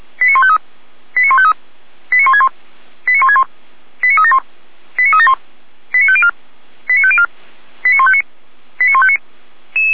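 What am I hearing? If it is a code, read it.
Five-tone selective call (ZVEI), heard as quick runs of five tones, tones of about 70 ms. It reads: 89153 89153 89161 89161 89371 89371 89504 89504 89159 89159
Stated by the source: ZVEI five-tone selective call 89003 received by SDR, from feuerwehr-orlamuende.org (fire brigade Orlamünde).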